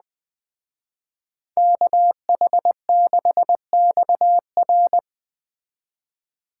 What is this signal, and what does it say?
Morse code, heard as a keyed tone, about 20 wpm, 700 Hz. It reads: KH6XR
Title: Speed Racing - USA 2x2 Call Signs 17wpm